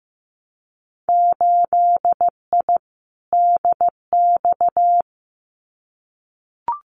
Morse code, heard as a keyed tone, about 15 wpm, 700 Hz. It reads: 8I DX
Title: Winter Field Day Exchanges Encore 15wpm